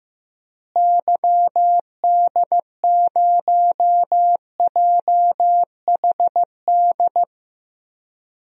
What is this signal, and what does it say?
Morse code, heard as a keyed tone, about 16 wpm, 700 Hz. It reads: YD0JHD
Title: DXCC-100 Call Signs - 6 Characters - Repeated 2x - 15wpm